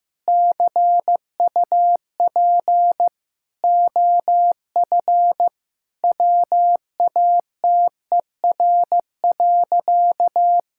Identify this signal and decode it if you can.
Morse code, heard as a keyed tone, about 15 wpm, 700 Hz. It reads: CUP OF WATER.
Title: Sentences from Top 1600 Words 15wpm